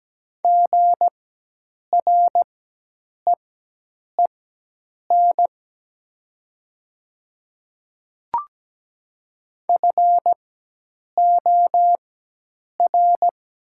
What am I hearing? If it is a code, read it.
Morse code, heard as a keyed tone, about 17 wpm, spaced as for 8 wpm, 700 Hz. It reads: GREEN FOR
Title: 4x Character Spacing - Top 500 Words - 17wpm